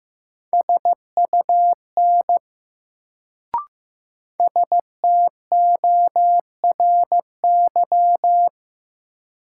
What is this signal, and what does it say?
Morse code, heard as a keyed tone, about 15 wpm, 700 Hz. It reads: SUN STORY
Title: Sentences from Top 300 Words - New Word Drill - 15wpm